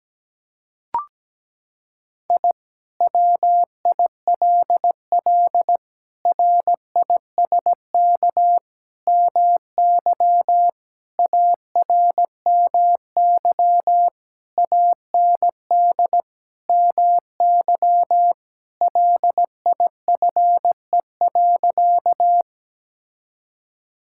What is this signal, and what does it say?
Morse code, heard as a keyed tone, about 17 wpm, 700 Hz. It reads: I WILL RISK MY ARMY AND MY LIFE.
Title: Sentences from Top 1000 Words - Review Part 5 - 17wpm